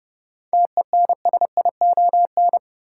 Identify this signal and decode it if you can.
Morse code, heard as a keyed tone, about 32 wpm, 700 Hz. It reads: TENSION
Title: Sentences from Top 1600 Words - New Word Drill - 30wpm